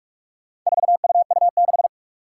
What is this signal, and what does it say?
Morse code, heard as a keyed tone, about 44 wpm, 700 Hz. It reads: 4UU6
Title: Four Challenging Letter-Number Combinations 45wpm